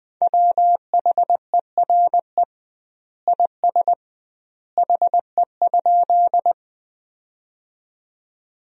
Morse code, heard as a keyed tone, about 21 wpm, 700 Hz. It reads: WHERE IS HE?
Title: 2x Word-Spacing - Sentences from Top 200 Words - 20wpm